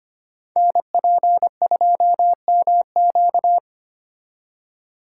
Morse code, heard as a keyed tone, about 25 wpm, 700 Hz. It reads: NP2MQ
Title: Code Groups - 5 Letter:Numbers - 25wpm